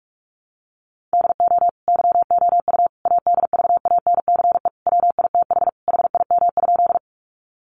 Morse code, heard as a keyed tone, about 44 wpm, 700 Hz. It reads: DO YOU ADVANCE WITH HIM?
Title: Sentences from Top 1400 Words 45wpm